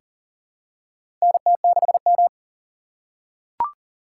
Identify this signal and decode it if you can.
Morse code, heard as a keyed tone, about 40 wpm, 700 Hz. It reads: NT6M
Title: USA Call Signs - 2x Character Spacing - 40wpm